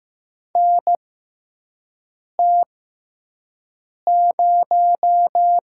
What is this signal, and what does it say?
Morse code, heard as a keyed tone, about 15 wpm, spaced as for 5 wpm, 700 Hz. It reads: NT0
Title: USA Call Signs - 6x Character Spacing - 15wpm